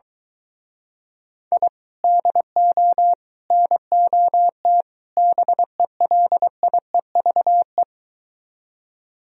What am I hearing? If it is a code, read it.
Morse code, heard as a keyed tone, about 23 wpm, 700 Hz. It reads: I DO NOT BELIEVE